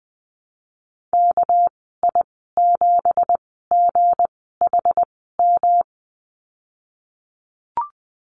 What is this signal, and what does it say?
Morse code, heard as a keyed tone, about 20 wpm, spaced as for 15 wpm, 700 Hz. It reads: KI7GHM
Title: 213 Lesson 34g - Number 8 - Call Signs 20wpm